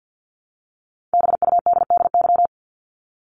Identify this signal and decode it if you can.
Morse code, heard as a keyed tone, about 49 wpm, 700 Hz. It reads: BUDDY
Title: Top 1751-2000 Words 50wpm